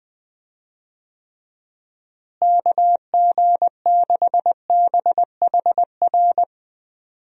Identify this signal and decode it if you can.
Morse code, heard as a keyed tone, about 20 wpm, 700 Hz. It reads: KG6BHR
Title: USA Call Signs - 6 Characters - Repeated 3x - 20wpm